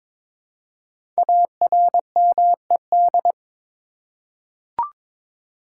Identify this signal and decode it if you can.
Morse code, heard as a keyed tone, about 22 wpm, 690 Hz. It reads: ARMED
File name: 5 Letter Words - First 3 Letters Given - 22wpm